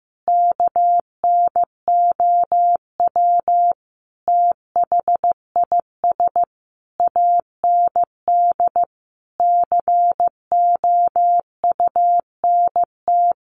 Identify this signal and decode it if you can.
Morse code, heard as a keyed tone, about 15 wpm, 700 Hz. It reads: KNOW THIS AND COUNT
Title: Sentences from Top 1500 Words 15wpm